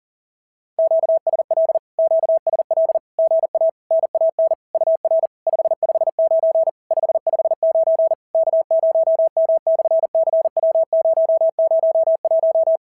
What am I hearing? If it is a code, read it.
Morse code, heard as a keyed tone, about 41 wpm, 650 Hz. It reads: QSL QSL GA NAN UR 559 559 K0M/CW001
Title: SOTA QSO Practice 40wpm